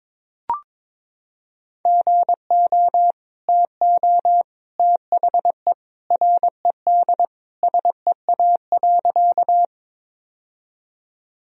Morse code, heard as a keyed tone, about 22 wpm, 700 Hz. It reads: GO TO THE RED SEA.
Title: Sentences from Top 1000 Words - Review Part 2 - 22wpm